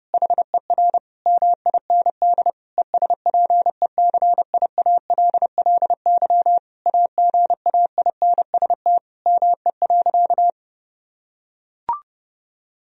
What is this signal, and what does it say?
Morse code, heard as a keyed tone, about 30 wpm, 700 Hz. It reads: HER MIND ESPECIALLY AGAINST ME.
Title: Sentences from Top 1000 Words - Review Part 4 - 30wpm